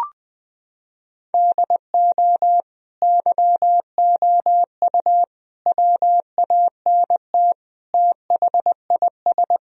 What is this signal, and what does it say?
Morse code, heard as a keyed tone, about 20 wpm, 700 Hz. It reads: DO YOU WANT THIS
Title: Sentences from Top 1000 Words - Review Part 2 - 20wpm